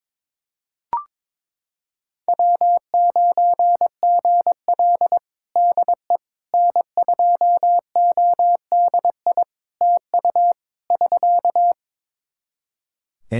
Morse code, heard as a keyed tone, about 22 wpm, 700 Hz. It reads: W9GL DE N2ODI TU <SK>
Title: Practice QSO - Exchange 4 - Ending - 22wpm